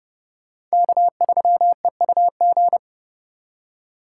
Morse code, heard as a keyed tone, about 32 wpm, 700 Hz. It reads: K3EUG
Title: USA Call Signs - 5 Characters - 3x Character Spacing - 30wpm